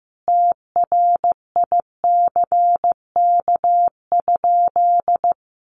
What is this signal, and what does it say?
Morse code, heard as a keyed tone, about 15 wpm, 700 Hz. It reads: TRICK?
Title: Sentences from Top 1500 Words 15wpm